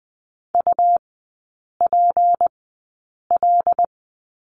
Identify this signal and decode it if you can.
Morse code, heard as a keyed tone, about 20 wpm, spaced as for 8 wpm, 700 Hz. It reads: UPL
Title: 152 Lesson 25e - Letter p - Three Characters 20wpm